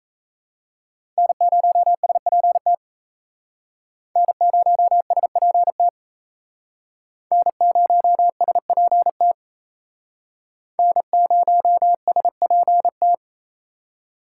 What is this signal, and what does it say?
Morse code, heard as a keyed tone, about 36 wpm, 700 Hz. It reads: N0SPT N0SPT N0SPT N0SPT